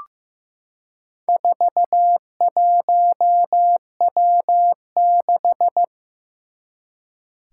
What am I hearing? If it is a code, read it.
Morse code, heard as a keyed tone, about 15 wpm, 700 Hz. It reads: 41W6
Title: Four Challenging Letter-Number Combinations 15wpm